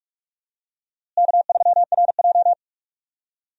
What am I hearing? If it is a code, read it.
Morse code, heard as a keyed tone, about 41 wpm, 700 Hz. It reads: K3RJ
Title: USA Call Signs - 4 Characters - Repeated 3x - 45wpm